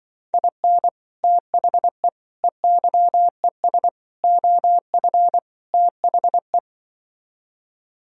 Morse code, heard as a keyed tone, about 24 wpm, 700 Hz. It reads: IN THE EYES OF THE
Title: Speed-Racing - Sets of 5 Words 20wpm